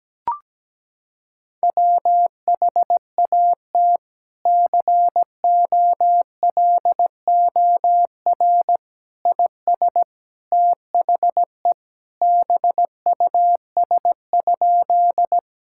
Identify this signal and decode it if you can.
Morse code, heard as a keyed tone, about 17 wpm, 700 Hz. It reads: WHAT COLOR IS THE BUS?
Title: Sentences from Top 1000 Words - Review Part 6 - 17wpm